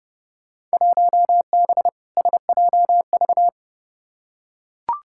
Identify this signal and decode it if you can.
Morse code, heard as a keyed tone, about 30 wpm, 700 Hz. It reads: 1B SJV
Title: Field Day Exchanges Encore - 30wpm